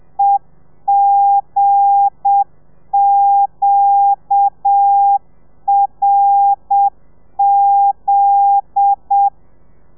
Morse code, heard as a keyed tone, about 7 wpm, 800 Hz. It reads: EGQRZ